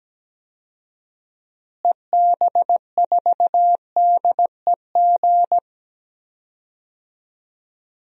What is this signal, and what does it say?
Morse code, heard as a keyed tone, about 17 wpm, 700 Hz. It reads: EB4DEG
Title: DXCC-100 Call Signs - 6 Characters - Repeated 2x - 17wpm